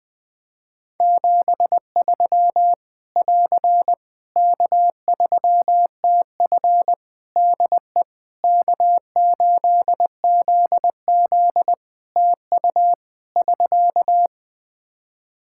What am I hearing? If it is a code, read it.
Morse code, heard as a keyed tone, about 20 wpm, 700 Hz. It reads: 73 <AR> K3TF DE K8ZZ TU <SK>
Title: K1USN SST QSO Practice 20wpm